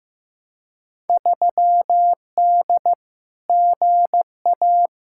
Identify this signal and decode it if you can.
Morse code, heard as a keyed tone, about 15 wpm, 700 Hz. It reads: 3D GA
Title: Field Day Exchanges Encore - 15wpm